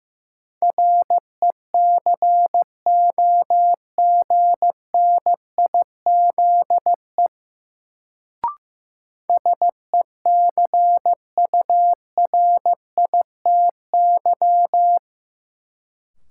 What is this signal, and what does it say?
Morse code, heard as a keyed tone, about 15 wpm, 700 Hz. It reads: RECOGNIZE SECURITY
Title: Sentences from Top 1400 Words - New Word Drill Alt - 15wpm